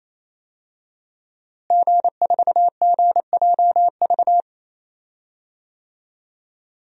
Morse code, heard as a keyed tone, about 28 wpm, 700 Hz. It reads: G4GJV
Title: DXCC-100 Call Signs - 5 Characters - Repeated 3x - 28wpm